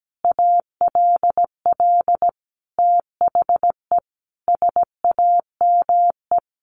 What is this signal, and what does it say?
Morse code, heard as a keyed tone, about 17 wpm, 700 Hz. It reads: ALL THE SAME
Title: Sentences from Top 600 Words 17wpm